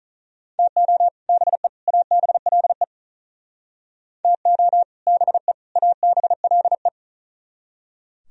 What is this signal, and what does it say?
Morse code, heard as a keyed tone, about 45 wpm, 700 Hz. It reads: TO BE ABLE TO BE ABLE